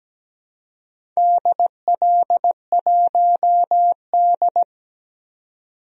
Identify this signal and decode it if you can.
Morse code, heard as a keyed tone, about 17 wpm, 700 Hz. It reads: DL1D
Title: DXCC-100 Call Signs - 4 Characters - 17wpm